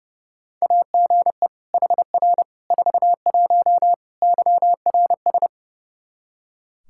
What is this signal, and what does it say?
Morse code, heard as a keyed tone, about 30 wpm, 700 Hz. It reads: AGE HR 41 YRS
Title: Practice QSO - Exchange 3 - More Info - 30wpm